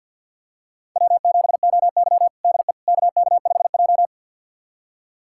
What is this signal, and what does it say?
Morse code, heard as a keyed tone, about 46 wpm, 700 Hz. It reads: W7OY DE KK5J
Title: USA Callsign DE Callsign - 4 Characters - 50wpm